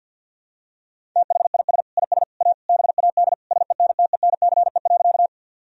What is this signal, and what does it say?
Morse code, heard as a keyed tone, about 52 wpm, 700 Hz. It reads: THIS IS A BAD SENTENCE.
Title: Sentences from Top 1000 Words - Review Part 1 - 50wpm